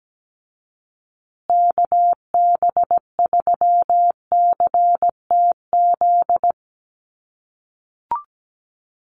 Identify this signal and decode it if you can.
Morse code, heard as a keyed tone, about 17 wpm, 700 Hz. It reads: KB3CTZ
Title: USA Call Signs Repeated 3x 17wpm